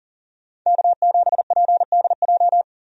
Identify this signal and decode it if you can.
Morse code, heard as a keyed tone, about 40 wpm, 700 Hz. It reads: K7PDJ
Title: USA Call Signs - 5 Characters - 40wpm